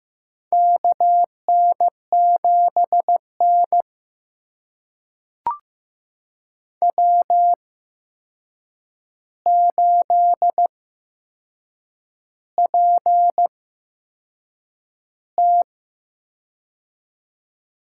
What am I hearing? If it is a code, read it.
Morse code, heard as a keyed tone, about 15 wpm, spaced as for 6 wpm, 700 Hz. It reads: KN7N W8PT
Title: USA Call Signs - 4 Characters - 8x Character Spacing - 15wpm